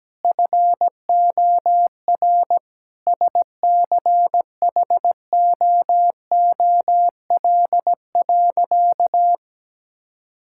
Morse code, heard as a keyed tone, about 17 wpm, 700 Hz. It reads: FOR SCHOOL.
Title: Sentences from Top 1000 Words - Review Part 6 - 17wpm